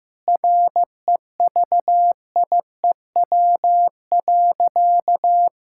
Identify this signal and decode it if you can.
Morse code, heard as a keyed tone, about 15 wpm, 700 Hz. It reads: REVIEW.
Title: Sentences from Top 1000 Words - Review Part 6 - 15wpm